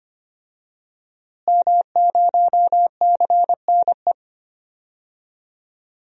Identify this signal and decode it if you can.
Morse code, heard as a keyed tone, about 25 wpm, 700 Hz. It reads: M0CNE
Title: DXCC-100 Call Signs - 5 Characters - Repeated 3x - 25wpm